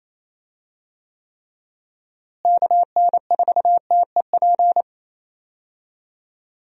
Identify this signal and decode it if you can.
Morse code, heard as a keyed tone, about 27 wpm, 700 Hz. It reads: KN4TEP
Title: USA Call Signs - 6 Characters - Repeated 3x - 28wpm